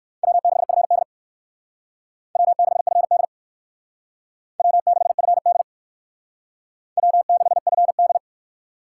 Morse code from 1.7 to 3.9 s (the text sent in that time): W6FD